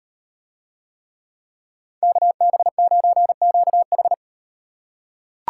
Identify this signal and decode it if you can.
Morse code, heard as a keyed tone, about 35 wpm, 700 Hz. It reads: KB9QH